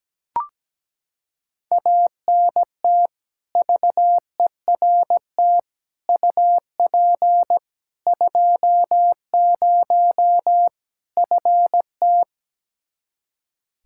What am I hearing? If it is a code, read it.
Morse code, heard as a keyed tone, about 17 wpm, 700 Hz. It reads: ANT VERT UP 20 FT
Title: Practice QSO - Exchange 2 - Info - 17wpm